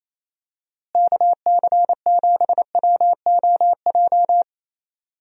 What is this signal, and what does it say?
Morse code, heard as a keyed tone, about 28 wpm, 700 Hz. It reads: KC7WOJ
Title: USA Call Signs - 6 Characters - Repeated 3x - 28wpm